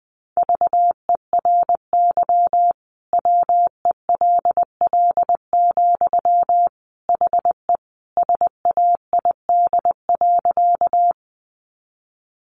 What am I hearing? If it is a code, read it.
Morse code, heard as a keyed tone, about 20 wpm, 700 Hz. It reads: VERY WELL, HE SAID.
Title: Sentences from Top 200 Words 20wpm